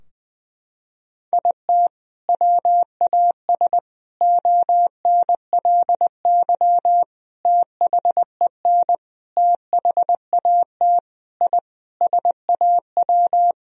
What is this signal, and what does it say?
Morse code, heard as a keyed tone, about 20 wpm, 700 Hz. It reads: IT WAS ONLY THEN THAT I SAW
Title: Sentences from Top 1000 Words - Review Part 3 - 20wpm